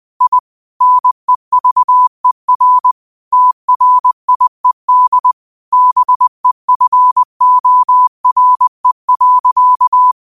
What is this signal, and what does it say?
Morse code, heard as a keyed tone, about 20 wpm, 1000 Hz. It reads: I NEVER TRIED BEFORE.